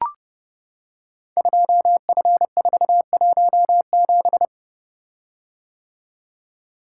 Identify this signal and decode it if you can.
Morse code, heard as a keyed tone, about 30 wpm, 700 Hz. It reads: 2F417
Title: Code Groups - 5 Letter:Numbers - 30wpm